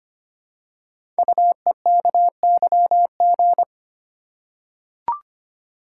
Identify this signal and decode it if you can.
Morse code, heard as a keyed tone, about 25 wpm, 700 Hz. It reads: UEKYG
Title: Code Groups - 5 Letters - 25wpm